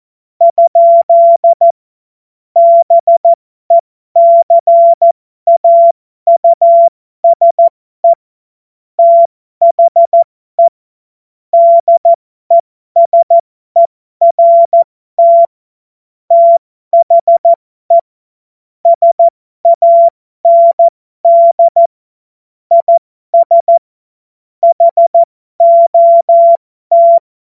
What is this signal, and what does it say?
Morse code, heard as a keyed tone, about 14 wpm, spaced as for 12 wpm, 680 Hz. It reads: ? BECAUSE THE DESERT THE SAND IS HOT